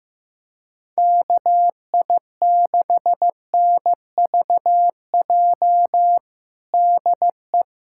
Morse code, heard as a keyed tone, about 15 wpm, 700 Hz. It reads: KI6NVJ DE
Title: USA Callsign DE Callsign - 6 Characters - 15wpm